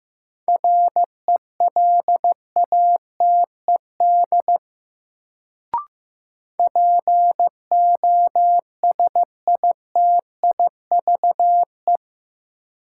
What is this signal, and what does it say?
Morse code, heard as a keyed tone, about 15 wpm, 700 Hz. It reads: RELATED POSITIVE
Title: Sentences from Top 1000 Words - Review Part 5 - New Word Drill - 15wpm